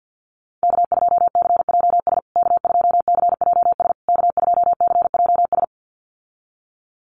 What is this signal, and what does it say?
Morse code, heard as a keyed tone, about 48 wpm, 700 Hz. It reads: K2CJS K2CJS K2CJS
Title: USA Call Signs Repeated 3x 50wpm